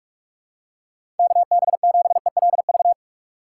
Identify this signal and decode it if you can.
Morse code, heard as a keyed tone, about 46 wpm, 700 Hz. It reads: KB7ELV
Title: USA Call Signs - 6 Characters - 3x Character Spacing - 45wpm